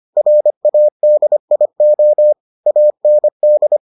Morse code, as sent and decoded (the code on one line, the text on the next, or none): .-. .- -.. .. --- / .- -. -..
RADIO AND